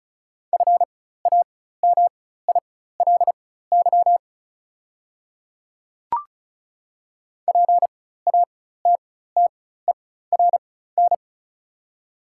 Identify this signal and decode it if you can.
Morse code, heard as a keyed tone, about 36 wpm, spaced as for 16 wpm, 700 Hz. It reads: FAMILY PATTERN
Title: 4x Character Spacing - Top 500 Words - 35wpm